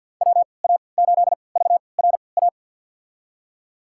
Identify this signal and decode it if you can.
Morse code, heard as a keyed tone, about 47 wpm, spaced as for 28 wpm, 700 Hz. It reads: WA8VRA
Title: USA Call Signs - 6 Characters - 3x Character Spacing - 50wpm